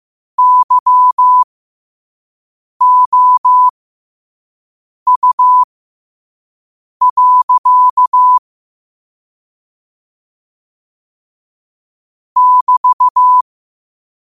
Morse code, heard as a keyed tone, about 15 wpm, spaced as for 5 wpm, 1000 Hz. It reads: YOU. <BT>
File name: qotdfiles / 5wpm